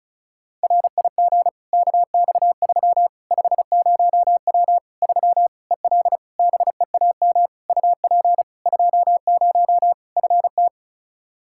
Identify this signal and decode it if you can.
Morse code, heard as a keyed tone, about 34 wpm, 700 Hz. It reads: RIG KX3 50W 3 EL BEAM UP 20 FT
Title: Practice QSO - Exchange 2 - Info - 35wpm